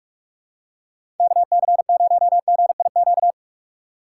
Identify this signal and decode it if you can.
Morse code, heard as a keyed tone, about 46 wpm, 700 Hz. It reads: KC0GIQ
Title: USA Call Signs - 6 Characters - 3x Character Spacing - 45wpm